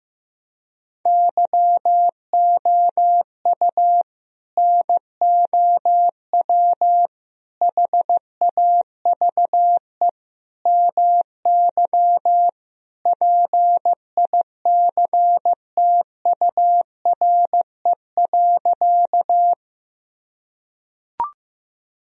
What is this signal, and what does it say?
Morse code, heard as a keyed tone, about 15 wpm, 700 Hz. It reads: YOU NOW HAVE MY PICTURE.
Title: Sentences from Top 1000 Words - Review Part 1 - 15wpm